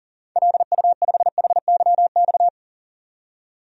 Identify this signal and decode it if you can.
Morse code, heard as a keyed tone, about 40 wpm, 700 Hz. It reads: LU5HYX
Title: DXCC-100 Call Signs - 6 Characters - 40wpm